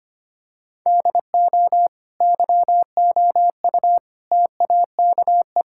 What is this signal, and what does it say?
Morse code, heard as a keyed tone, about 25 wpm, 700 Hz. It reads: DO YOU TAKE
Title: Sentences from Top 1000 Words - Review Part 6 - 25wpm